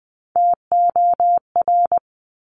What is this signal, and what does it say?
Morse code, heard as a keyed tone, about 20 wpm, 700 Hz. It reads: TOR
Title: Top Three Letter Combinations in Words 20wpm